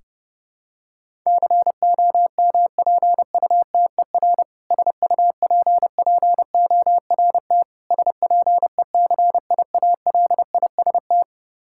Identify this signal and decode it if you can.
Morse code, heard as a keyed tone, about 31 wpm, 700 Hz. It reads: COMPUTER SUPPORT SPECIALIST
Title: Practice QSO - Exchange 3 - More Info - 30wpm